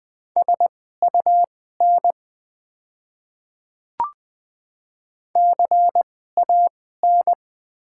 Morse code, heard as a keyed tone, about 20 wpm, spaced as for 14 wpm, 690 Hz. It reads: SUN CAN
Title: 2x Character Spacing - Top 500 Words - 20wpm